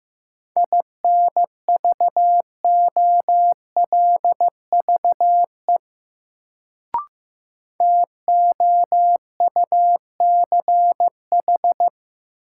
Morse code, heard as a keyed tone, about 15 wpm, 700 Hz. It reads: INVOLVE TOUCH
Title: Sentences from Top 1200 Words - New Word Drill - 15wpm